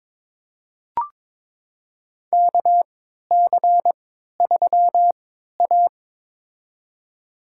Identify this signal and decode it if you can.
Morse code, heard as a keyed tone, about 22 wpm, spaced as for 12 wpm, 700 Hz. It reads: KC3A
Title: USA Call Signs - 4 Characters - 3x Character Spacing - 22wpm